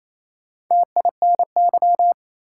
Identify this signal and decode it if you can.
Morse code, heard as a keyed tone, about 27 wpm, 700 Hz. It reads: TINY
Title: Four Letter Words - 28wpm